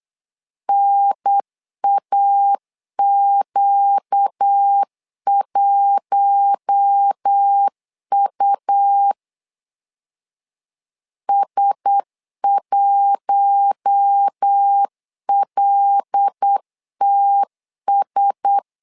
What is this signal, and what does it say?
Morse code, heard as a keyed tone, about 8 wpm, 800 Hz. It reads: NAQ1U S1LTS
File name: LEZIONI / 5 LBVC / 5 LB 8wpm 2816